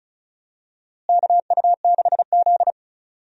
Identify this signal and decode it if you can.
Morse code, heard as a keyed tone, about 36 wpm, 700 Hz. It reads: KU6Z
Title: USA Call Signs - 4 Characters - Repeated 3x - 35wpm